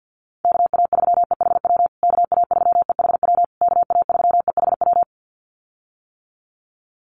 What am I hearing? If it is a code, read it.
Morse code, heard as a keyed tone, about 50 wpm, 700 Hz. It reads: KA3EHW KA3EHW KA3EHW
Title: USA Call Signs Repeated 3x 50wpm